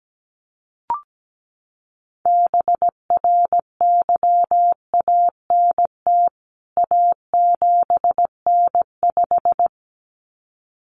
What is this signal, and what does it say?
Morse code, heard as a keyed tone, about 17 wpm, 700 Hz. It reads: BRYANT A7N5
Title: CWT Exchange 17wpm